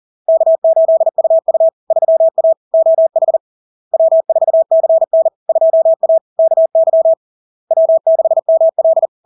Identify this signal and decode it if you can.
Morse code, heard as a keyed tone, about 40 wpm, 650 Hz. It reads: K8UU 3A OH W4CN 2A KY W6ML
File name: LARK-FD-2022-Log-in-CW-40wpm-650-Hz